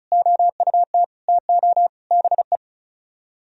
Morse code, heard as a keyed tone, about 37 wpm, 700 Hz. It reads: OUT TO BE
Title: Speed-Racing - Sets of 5 Words 35wpm